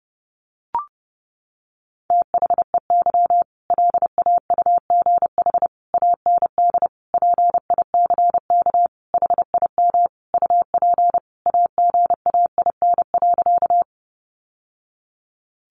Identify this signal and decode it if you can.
Morse code, heard as a keyed tone, about 30 wpm, 700 Hz. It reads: THEY LAUGH AND PICK HIM UP AGAIN.
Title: Sentences from Top 1000 Words 30wpm